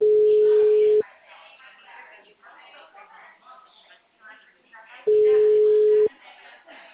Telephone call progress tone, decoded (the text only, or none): ringback tone